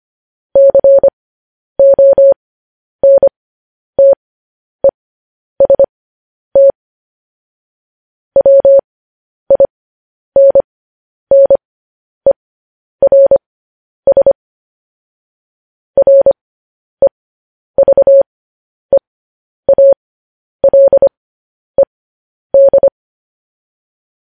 Morse code, heard as a keyed tone, about 25 wpm, spaced as for 10 wpm, 550 Hz. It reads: CONTEST WINNERS REVEALED
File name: podcast / News headlines-1x-25x10